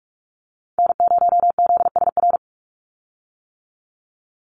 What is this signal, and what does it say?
Morse code, heard as a keyed tone, about 44 wpm, 700 Hz. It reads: N0ZSR
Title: USA Call Signs 45wpm